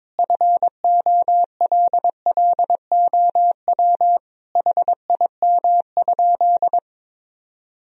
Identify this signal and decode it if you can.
Morse code, heard as a keyed tone, about 22 wpm, 700 Hz. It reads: FOLLOW HIM?
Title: Sentences from Top 1000 Words - Review Part 1 - 22wpm